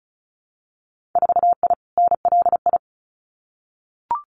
Morse code, heard as a keyed tone, about 36 wpm, 700 Hz. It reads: 4I NLI
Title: Winter Field Day Exchanges Encore 35wpm